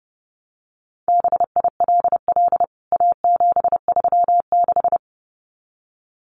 Morse code, heard as a keyed tone, about 30 wpm, 700 Hz. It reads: BILL A736
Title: CWT Exchange 30wpm